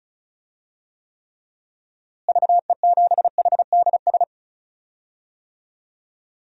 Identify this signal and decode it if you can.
Morse code, heard as a keyed tone, about 36 wpm, 700 Hz. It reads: VE7HDS
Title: DXCC-100 Call Signs - 6 Characters - Repeated 2x - 35wpm